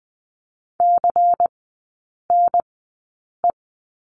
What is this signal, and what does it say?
Morse code, heard as a keyed tone, about 20 wpm, spaced as for 8 wpm, 700 Hz. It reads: CNE
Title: 093 Lesson 16e - Letter c - Three Characters 20wpm